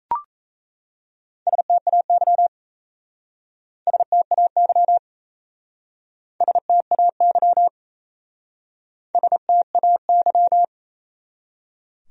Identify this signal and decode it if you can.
Morse code, heard as a keyed tone, about 34 wpm, 700 Hz. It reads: STAY STAY STAY STAY
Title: Speed-Racing Top 500 Words 28wpm